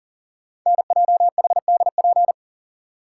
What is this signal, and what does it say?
Morse code, heard as a keyed tone, about 42 wpm, 700 Hz. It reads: NJHDP
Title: Code Groups - 5 Letters - 40wpm